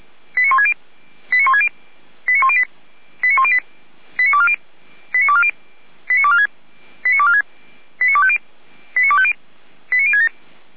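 Five-tone selective call (ZVEI), heard as quick runs of five tones, tones of about 70 ms. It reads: 89160 89160 89108 89108 89240 89240 89246 89246 89250 89250 89067
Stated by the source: ZVEI five-tone call 89336 received via SDR.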